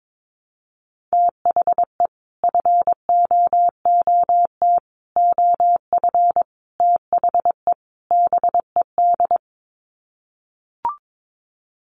Morse code, heard as a keyed tone, about 22 wpm, 700 Hz. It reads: THE FOOT OF THE BED
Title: Sets of 5 Words Encore2 22wpm